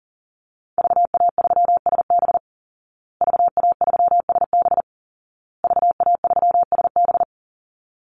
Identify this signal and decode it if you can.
Morse code, heard as a keyed tone, about 40 wpm, 700 Hz. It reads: VA3SB VA3SB VA3SB
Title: Call Signs Repeated 2x - 40wpm